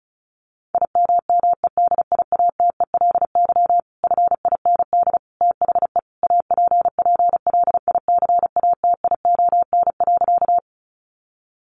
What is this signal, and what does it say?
Morse code, heard as a keyed tone, about 34 wpm, 700 Hz. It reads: IMMEDIATELY FIND THE APPLICATION.
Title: Sentences from Top 1200 Words 35wpm